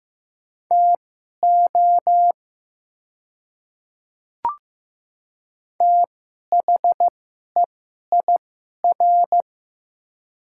Morse code, heard as a keyed tone, about 15 wpm, spaced as for 11 wpm, 700 Hz. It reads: TO THEIR
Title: 2x Character Spacing - Top 100 Words - 15wpm